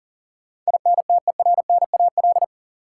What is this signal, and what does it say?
Morse code, heard as a keyed tone, about 40 wpm, 700 Hz. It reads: INTERNAL
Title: Sentences from Top 1400 Words - New Word Drill - 40wpm